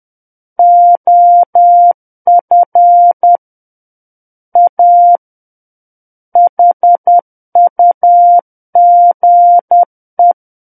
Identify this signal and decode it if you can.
Morse code, heard as a keyed tone, about 10 wpm, 700 Hz. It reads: OF A HUGE
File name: Book One / 01-Chapter 1 - 10 WPM